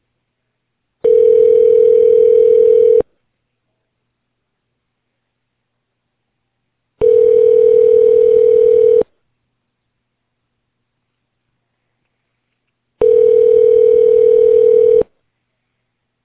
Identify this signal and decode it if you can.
Ringback tone (North America), 440 + 480 Hz.